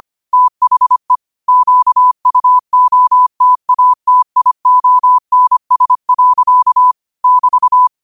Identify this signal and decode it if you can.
Morse code, heard as a keyed tone, about 25 wpm, 1000 Hz. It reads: THE QUOTATIONS. <BT>